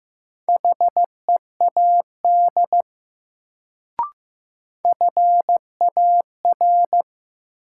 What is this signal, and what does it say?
Morse code, heard as a keyed tone, about 15 wpm, 700 Hz. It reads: HEAD FAR
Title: Sets of 3 Words - New Word Drill - 15wpm